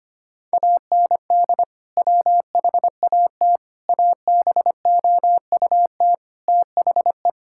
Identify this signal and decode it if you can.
Morse code, heard as a keyed tone, about 25 wpm, 700 Hz. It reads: AND WHAT ABOUT THE
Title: Sentences from Top 1000 Words - Review Part 5 - 25wpm